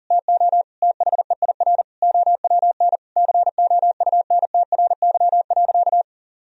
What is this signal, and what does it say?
Morse code, heard as a keyed tone, about 40 wpm, 700 Hz. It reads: TO THEIR OWN COUNTRY.
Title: Sentences from Top 1000 Words - Review Part 1 - 40wpm